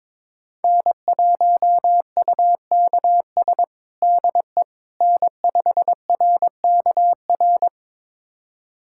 Morse code, heard as a keyed tone, about 22 wpm, 700 Hz. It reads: N1UKS DE N5RKR
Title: USA Callsign DE Callsign - 5 Characters - 22wpm